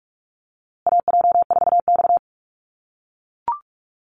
Morse code, heard as a keyed tone, about 44 wpm, 700 Hz. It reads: AJ4X
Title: USA Call Signs Repeated 2x 45wpm